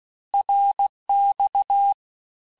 Morse code, heard as a keyed tone, about 16 wpm, 800 Hz. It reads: RX
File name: cours-du-lundi files / Cours-1586-texte-abreviations-vitesse-15mpm-puis-20mpm